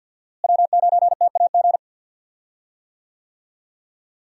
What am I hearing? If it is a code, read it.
Morse code, heard as a keyed tone, about 52 wpm, 700 Hz. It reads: W9TAG
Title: USA Call Signs - 5 Characters - Repeated 3x - 50wpm